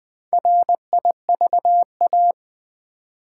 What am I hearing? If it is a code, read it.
Morse code, heard as a keyed tone, about 20 wpm, 700 Hz. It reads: RIVA
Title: Build A Word - 1 Letter At A Time - 5-Letter Words - 20wpm